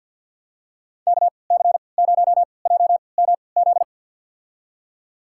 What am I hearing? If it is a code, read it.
Morse code, heard as a keyed tone, about 53 wpm, spaced as for 29 wpm, 700 Hz. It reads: KC0JMZ
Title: USA Call Signs - 6 Characters - 3x Character Spacing - 50wpm